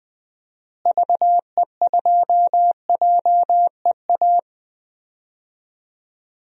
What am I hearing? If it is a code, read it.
Morse code, heard as a keyed tone, about 20 wpm, 700 Hz. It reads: VE2JEA
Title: DXCC-100 Call Signs - 6 Characters - Repeated 2x - 20wpm